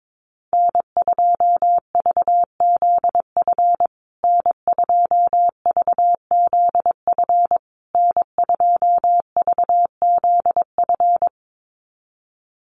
Morse code, heard as a keyed tone, about 22 wpm, 700 Hz. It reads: N2VZF N2VZF N2VZF
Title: USA Call Signs Repeated 3x 22wpm